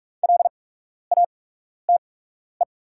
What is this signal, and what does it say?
Morse code, heard as a keyed tone, about 45 wpm, spaced as for 12 wpm, 700 Hz. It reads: LATE
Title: 8x Character Spacing - Top 300 Words - 45wpm